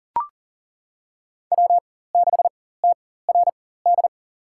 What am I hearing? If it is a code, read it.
Morse code, heard as a keyed tone, about 40 wpm, spaced as for 19 wpm, 700 Hz. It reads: W6TRD
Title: USA Call Signs - 5 Characters - 4x Character Spacing - 40wpm